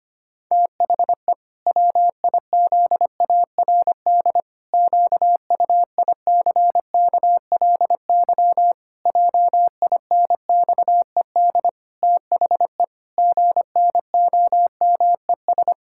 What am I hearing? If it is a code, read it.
Morse code, heard as a keyed tone, about 25 wpm, 700 Hz. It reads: THE WIZARD QUICKLY JINXED THE GNOMES
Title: Pangram Challenge v2 - 25wpm